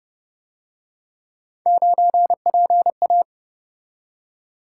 Morse code, heard as a keyed tone, about 30 wpm, 700 Hz. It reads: 9PA